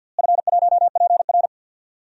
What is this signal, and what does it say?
Morse code, heard as a keyed tone, about 51 wpm, 700 Hz. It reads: F1PR
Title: DXCC-100 Call Signs - 4 Characters - Repeated 3x - 50wpm